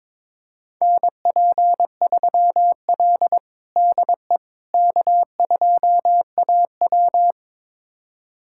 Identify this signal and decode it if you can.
Morse code, heard as a keyed tone, about 22 wpm, 700 Hz. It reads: NP3L DE K2AW
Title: USA Callsign DE Callsign - 4 Characters - 22wpm